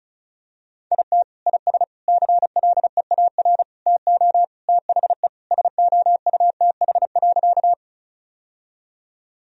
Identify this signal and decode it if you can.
Morse code, heard as a keyed tone, about 34 wpm, 700 Hz. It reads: IT IS CLEAR TO THE SOUTH.
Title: Sentences from Top 1000 Words - Review Part 2 - 35wpm